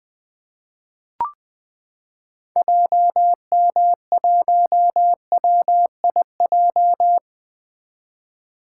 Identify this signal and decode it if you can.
Morse code, heard as a keyed tone, about 20 wpm, 700 Hz. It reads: JM1WIJ
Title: 2023 POTA Activator Call Signs - 20wpm